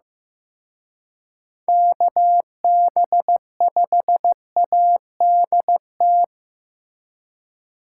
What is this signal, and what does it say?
Morse code, heard as a keyed tone, about 18 wpm, 700 Hz. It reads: KB5ADT
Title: Speed Racing - USA 2x3 Call Signs 15wpm